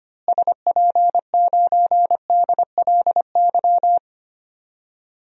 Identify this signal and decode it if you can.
Morse code, heard as a keyed tone, about 25 wpm, 700 Hz. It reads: SP9DLY
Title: DXCC-100 Call Signs - 6 Characters - Repeated 3x - 25wpm